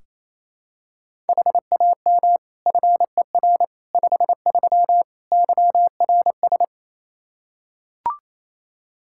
Morse code, heard as a keyed tone, about 28 wpm, 700 Hz. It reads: HAM FER 53 YRS
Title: Practice QSO - Exchange 3 - More Info - 28wpm